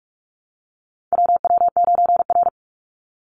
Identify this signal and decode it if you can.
Morse code, heard as a keyed tone, about 44 wpm, 700 Hz. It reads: WW9R
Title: Call Signs Repeated 3x - 45wpm